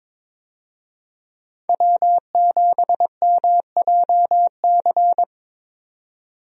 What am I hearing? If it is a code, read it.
Morse code, heard as a keyed tone, about 22 wpm, 700 Hz. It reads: W7MJC
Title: USA Call Signs - 5 Characters - 22wpm